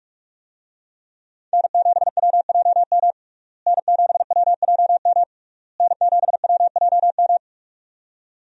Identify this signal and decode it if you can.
Morse code, heard as a keyed tone, about 45 wpm, 700 Hz. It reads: N7WJM N7WJM N7WJM
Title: USA Call Signs - 5 Characters - Repeated 3x - 45wpm